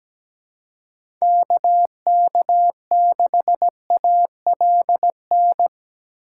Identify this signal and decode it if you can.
Morse code, heard as a keyed tone, about 17 wpm, 700 Hz. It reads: KK6ALN
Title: USA Call Signs - 6 Characters - 6x Character Spacing - 17wpm